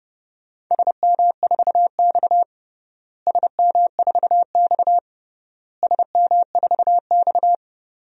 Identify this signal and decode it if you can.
Morse code, heard as a keyed tone, about 30 wpm, 700 Hz. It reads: SM4X SM4X SM4X
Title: DXCC-100 Call Signs - 4 Characters - Repeated 3x - 30wpm